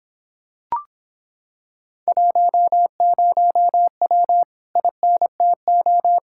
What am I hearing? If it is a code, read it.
Morse code, heard as a keyed tone, about 26 wpm, 700 Hz. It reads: 10W INTO